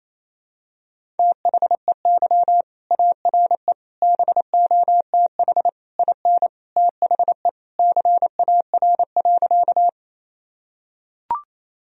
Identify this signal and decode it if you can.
Morse code, heard as a keyed tone, about 28 wpm, 700 Hz. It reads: THEY ARE BOTH IN THE CAR.
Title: Sentences from Top 1000 Words - Review Part 1 - 28wpm